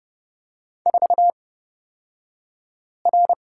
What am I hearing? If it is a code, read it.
Morse code, heard as a keyed tone, about 31 wpm, spaced as for 5 wpm, 700 Hz. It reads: 4R
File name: Lesson 38d - Letter q - Two Characters 30wpm